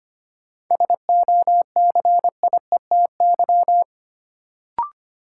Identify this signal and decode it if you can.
Morse code, heard as a keyed tone, about 25 wpm, 700 Hz. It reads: SOCIETY
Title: Sentences from Top 700 Words - New Word Drill - 25wpm